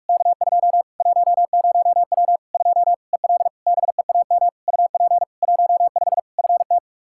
Morse code, heard as a keyed tone, about 43 wpm, 700 Hz. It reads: K2 10W 2 EL BEAM UP 15 FT
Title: Practice QSO - Exchange 2 - Info - 45wpm